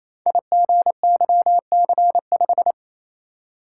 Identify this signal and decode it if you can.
Morse code, heard as a keyed tone, about 27 wpm, 700 Hz. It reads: IGYC5
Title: Code Groups - 5 Letter:Numbers - 28wpm